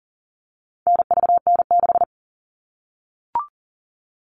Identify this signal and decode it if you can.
Morse code, heard as a keyed tone, about 40 wpm, 700 Hz. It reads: NVN6
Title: All Cut Numbers - 4-Digit 40wpm